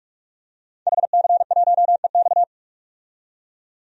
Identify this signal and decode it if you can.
Morse code, heard as a keyed tone, about 46 wpm, 700 Hz. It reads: HC1EX
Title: DXCC-100 Call Signs - 5 Characters - 45wpm